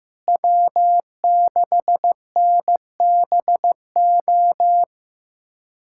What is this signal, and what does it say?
Morse code, heard as a keyed tone, about 15 wpm, 700 Hz. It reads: W6NBO
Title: USA Callsign DE Callsign - 5 Characters - 15wpm